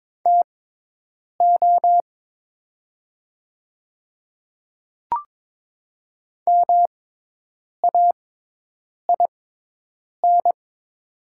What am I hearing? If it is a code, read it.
Morse code, heard as a keyed tone, about 22 wpm, spaced as for 8 wpm, 700 Hz. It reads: TO MAIN
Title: 6x Character Spacing - Top 300 Words - 22wpm